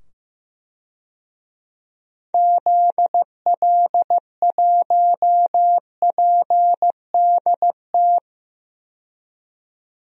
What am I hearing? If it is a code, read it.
Morse code, heard as a keyed tone, about 15 wpm, 700 Hz. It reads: ZL1PDT
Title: DXCC-100 Call Signs - 6 Characters - 15wpm